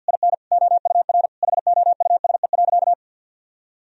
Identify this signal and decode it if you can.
Morse code, heard as a keyed tone, about 51 wpm, 700 Hz. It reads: IN OUR HOUSE.
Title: Sentences from Top 1000 Words - Review Part 1 - 50wpm